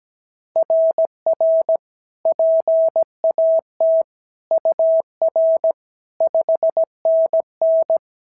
Morse code, heard as a keyed tone, about 16 wpm, 650 Hz. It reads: RR PAT UR 5NN
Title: SOTA QSO Practice 17wpm